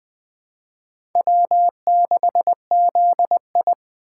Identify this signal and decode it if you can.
Morse code, heard as a keyed tone, about 20 wpm, 700 Hz. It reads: W6ZI
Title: USA Call Signs - 4 Characters - 4x Character Spacing - 20wpm